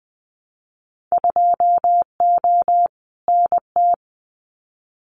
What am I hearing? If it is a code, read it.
Morse code, heard as a keyed tone, about 20 wpm, 700 Hz. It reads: 2O NT
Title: Winter Field Day Exchanges Encore 20wpm